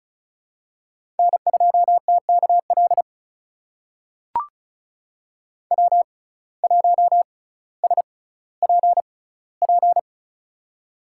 Morse code, heard as a keyed tone, about 36 wpm, spaced as for 18 wpm, 700 Hz. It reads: N2TKL W1SPP
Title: USA Call Signs - 5 Characters - 6x Character Spacing - 35wpm